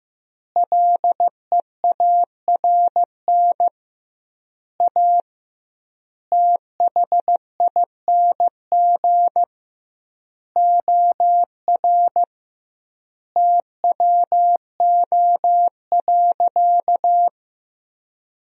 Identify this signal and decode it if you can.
Morse code, heard as a keyed tone, about 15 wpm, 700 Hz. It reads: LEARN A THING OR TWO.
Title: 2x Word-Spacing - Sentences from Top 300 Words - 15wpm